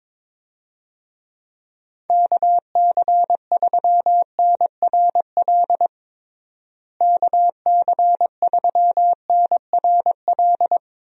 Morse code, heard as a keyed tone, about 21 wpm, 700 Hz. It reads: KC3NRL KC3NRL
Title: USA Call Signs - 6 Characters - Repeated 3x - 22wpm